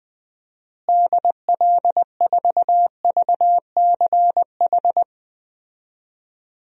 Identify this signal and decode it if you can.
Morse code, heard as a keyed tone, about 20 wpm, 700 Hz. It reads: DL4VCH
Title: DXCC-100 Call Signs - 6 Characters - Repeated 2x - 20wpm